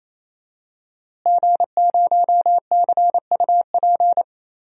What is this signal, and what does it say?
Morse code, heard as a keyed tone, about 28 wpm, 700 Hz. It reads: G0CUP
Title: DXCC-100 Call Signs - 5 Characters - Repeated 3x - 28wpm